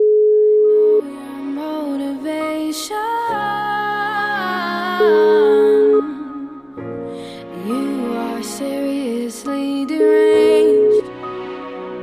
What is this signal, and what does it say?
Ringback tone (Europe, 425 Hz).